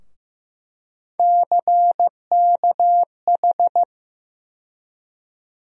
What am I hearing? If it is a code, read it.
Morse code, heard as a keyed tone, about 15 wpm, 700 Hz. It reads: CKH
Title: USA Call Sign Postfixes - 3 Letters 15wpm